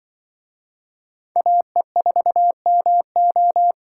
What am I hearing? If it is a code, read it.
Morse code, heard as a keyed tone, about 24 wpm, 700 Hz. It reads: AE4MO
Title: Speed Racing - USA 2x2 Call Signs 20wpm